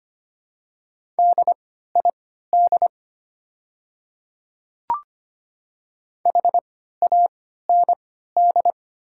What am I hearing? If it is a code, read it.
Morse code, heard as a keyed tone, about 25 wpm, spaced as for 14 wpm, 700 Hz. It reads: DID HAND
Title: 3x Character Spacing - Top 200 Words - 25wpm